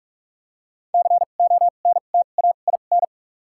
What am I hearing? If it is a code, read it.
Morse code, heard as a keyed tone, about 48 wpm, spaced as for 33 wpm, 700 Hz. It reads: CONTAIN
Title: 2x Character Spacing - Top 500 Words - 45wpm